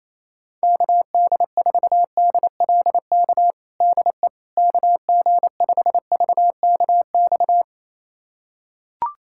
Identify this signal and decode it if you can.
Morse code, heard as a keyed tone, about 28 wpm, 700 Hz. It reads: KD4DLK DE KG5VKX
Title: USA Callsign DE Callsign - 6 Characters - 28wpm